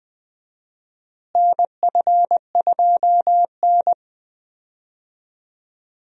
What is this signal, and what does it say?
Morse code, heard as a keyed tone, about 20 wpm, 700 Hz. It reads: NF2N
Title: USA Call Signs - 4 Characters - Repeated 3x - 20wpm